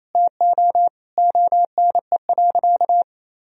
Morse code, heard as a keyed tone, about 28 wpm, 700 Hz. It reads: TO ONE.
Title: Sentences from Top 1000 Words - Review Part 3 - 28wpm